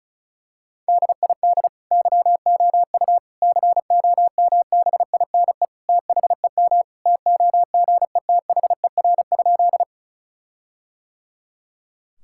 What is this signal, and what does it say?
Morse code, heard as a keyed tone, about 36 wpm, 700 Hz. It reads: DID YOU COMBINE THEM TOGETHER?
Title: Sentences from Top 1600 Words 35wpm